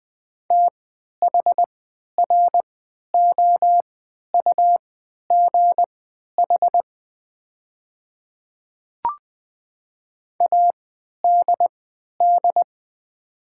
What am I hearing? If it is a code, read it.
Morse code, heard as a keyed tone, about 20 wpm, spaced as for 11 wpm, 700 Hz. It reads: THROUGH ADD
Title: 3x Character Spacing - Top 200 Words - 20wpm